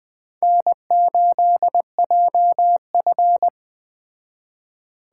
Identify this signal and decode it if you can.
Morse code, heard as a keyed tone, about 21 wpm, 700 Hz. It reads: N8JF
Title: Speed Racing - USA Call Signs 17wpm